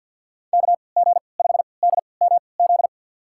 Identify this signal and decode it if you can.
Morse code, heard as a keyed tone, about 47 wpm, spaced as for 28 wpm, 700 Hz. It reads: KG5DMZ
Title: USA Call Signs - 6 Characters - 3x Character Spacing - 50wpm